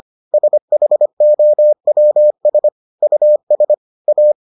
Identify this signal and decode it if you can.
Morse code, heard as a keyed tone, about 25 wpm, 600 Hz. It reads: SHOWS US A